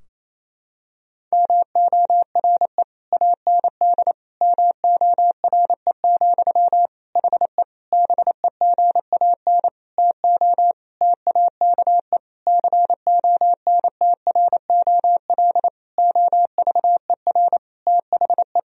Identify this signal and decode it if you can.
Morse code, heard as a keyed tone, about 28 wpm, 700 Hz. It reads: MORE AND MORE, HE BEGAN TO TAKE CONTROL OVER THE
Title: Sentences from Top 1000 Words - Review Part 3 - 28wpm